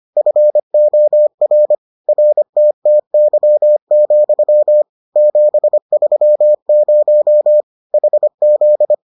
Morse code, heard as a keyed tone, about 25 wpm, 600 Hz. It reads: FOR RTTY, 730 HZ